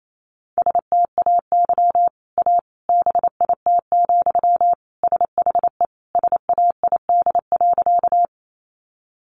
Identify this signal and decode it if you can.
Morse code, heard as a keyed tone, about 28 wpm, 700 Hz. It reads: STAY A BIT, SHE SAID.
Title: Sentences from Top 900 Words 28wpm